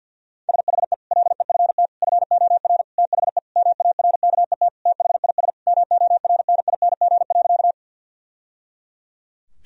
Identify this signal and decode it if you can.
Morse code, heard as a keyed tone, about 50 wpm, 700 Hz. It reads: SHE LEFT FOR THE MARKET THIS MORNING.
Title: Sentences from Top 1000 Words - Review Part 3 - 50wpm